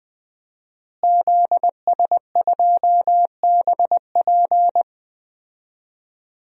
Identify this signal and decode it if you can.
Morse code, heard as a keyed tone, about 20 wpm, 700 Hz. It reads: ZS2BP
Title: DXCC-100 Call Signs - 5 Characters - Repeated 2x - 20wpm